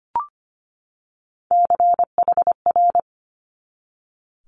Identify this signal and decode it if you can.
Morse code, heard as a keyed tone, about 25 wpm, 700 Hz. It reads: CHR
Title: Multi-letter Phonemes 25wpm